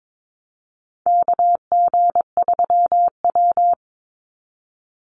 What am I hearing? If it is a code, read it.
Morse code, heard as a keyed tone, about 22 wpm, 700 Hz. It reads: KG3W
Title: NA SOTA Chaser Callsigns 22wpm